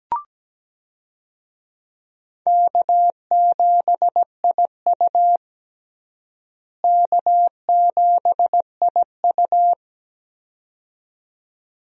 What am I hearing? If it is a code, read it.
Morse code, heard as a keyed tone, about 17 wpm, 700 Hz. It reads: K7IU K7IU
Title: USA Call Signs - 4 Characters - Repeated 2x - 17wpm